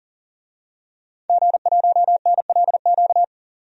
Morse code, heard as a keyed tone, about 40 wpm, 700 Hz. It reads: G1NLQ
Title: DXCC-100 Call Signs - 5 Characters - 40wpm